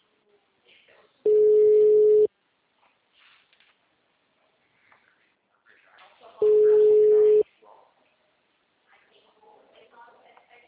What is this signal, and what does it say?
Ringback tone (Europe, 425 Hz), with speech.